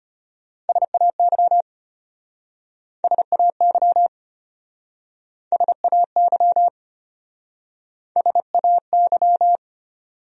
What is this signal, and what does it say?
Morse code, heard as a keyed tone, about 33 wpm, 700 Hz. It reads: SAY SAY SAY SAY